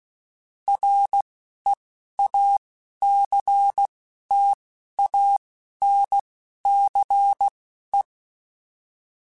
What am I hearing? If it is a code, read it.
Morse code, heard as a keyed tone, about 16 wpm, spaced as for 11 wpm, 800 Hz. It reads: REACTANCE
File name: Technical Subjects / Technical Subjects-mp3s / XC @10